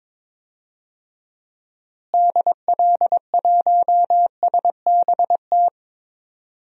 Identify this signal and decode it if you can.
Morse code, heard as a keyed tone, about 22 wpm, 700 Hz. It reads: DL1SBT
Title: DXCC-100 Call Signs - 6 Characters - Repeated 3x - 22wpm